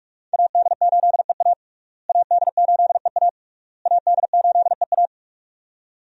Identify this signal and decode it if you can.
Morse code, heard as a keyed tone, about 48 wpm, 700 Hz. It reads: AD8EA AD8EA AD8EA
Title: USA Call Signs - 5 Characters - Repeated 3x - 45wpm